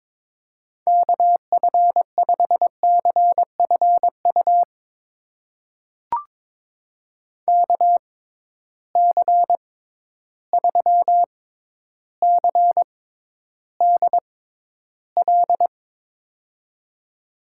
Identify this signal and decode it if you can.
Morse code, heard as a keyed tone, about 22 wpm, spaced as for 11 wpm, 700 Hz. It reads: KF5CFU KC3CDL
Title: USA Call Signs - 6 Characters - 6x Character Spacing - 22wpm